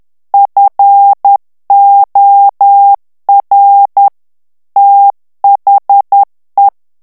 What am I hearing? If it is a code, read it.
Morse code, heard as a keyed tone, about 10 wpm, 800 Hz. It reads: FOR THE
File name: ANGLAIS texte clair 3 vitesse 10 mots minute